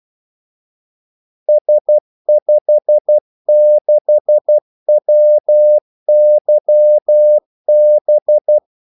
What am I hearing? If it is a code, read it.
Morse code, heard as a keyed tone, about 12 wpm, 600 Hz. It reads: S56WYB
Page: 2 ekstra ordmellemrum mellem kaldesignalerne